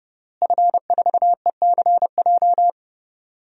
Morse code, heard as a keyed tone, about 30 wpm, 700 Hz. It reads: F4ECJ
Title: DXCC-100 Call Signs - 5 Characters - Repeated 2x - 30wpm